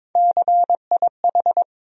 Morse code, heard as a keyed tone, about 21 wpm, 700 Hz. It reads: CIH
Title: USA Call Sign Postfixes - 3 Letters 22wpm